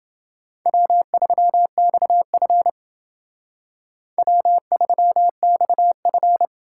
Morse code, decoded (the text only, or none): W3XF W3XF